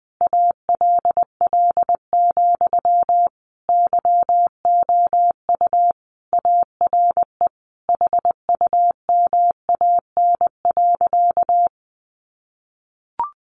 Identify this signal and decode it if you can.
Morse code, heard as a keyed tone, about 20 wpm, 700 Hz. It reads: ALL, YOU ARE HUMAN.
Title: Sentences from Top 600 Words 20wpm